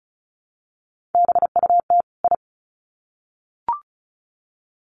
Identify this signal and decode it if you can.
Morse code, heard as a keyed tone, about 34 wpm, 700 Hz. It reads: BUT I
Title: Sets of 2 Words 35wpm